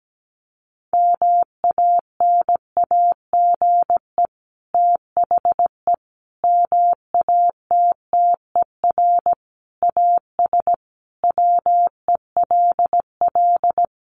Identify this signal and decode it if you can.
Morse code, heard as a keyed tone, about 17 wpm, 700 Hz. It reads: MANAGE THE MATTER AS WELL
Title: Sentences from Top 1400 Words 17wpm